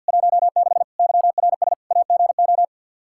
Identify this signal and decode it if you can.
Morse code, heard as a keyed tone, about 51 wpm, 700 Hz. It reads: 16 YRS AGO
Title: Practice QSO - Exchange 3 - More Info - 50wpm